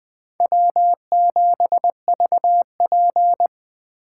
Morse code, heard as a keyed tone, about 20 wpm, 700 Hz. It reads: W7VP
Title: K1USN SST QSO Practice - 20wpm